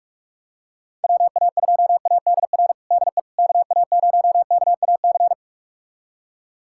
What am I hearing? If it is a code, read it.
Morse code, heard as a keyed tone, about 42 wpm, 700 Hz. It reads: WA2ADR DE KA0KAC
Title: USA Callsign DE Callsign - 6 Characters - 45wpm